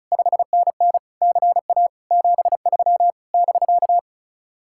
Morse code, heard as a keyed tone, about 45 wpm, 700 Hz. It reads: 5NN CA 73 <BK>